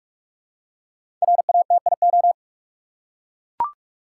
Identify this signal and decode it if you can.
Morse code, heard as a keyed tone, about 43 wpm, 700 Hz. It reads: RATIO
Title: Five Letter Words - 45wpm